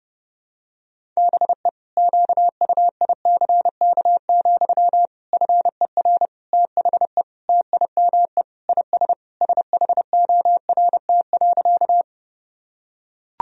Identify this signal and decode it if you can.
Morse code, heard as a keyed tone, about 30 wpm, 700 Hz. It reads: BE QUICK, FER THE TIME IS SHORT.
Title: Sentences from Top 500 Words and Abbreviations 30wpm